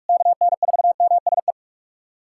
Morse code, heard as a keyed tone, about 42 wpm, 700 Hz. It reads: KN4MSE
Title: USA Callsign DE Callsign 45wpm